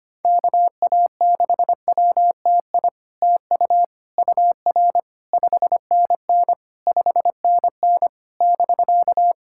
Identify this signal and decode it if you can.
Morse code, heard as a keyed tone, about 27 wpm, 700 Hz. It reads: KA6WTI TU UR 5NN 5NN <BK>
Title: SOTA QSO Practice 25wpm